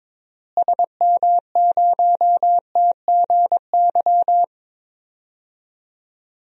Morse code, heard as a keyed tone, about 21 wpm, 700 Hz. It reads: SM0TGY
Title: DXCC-100 Call Signs - 6 Characters - Repeated 2x - 22wpm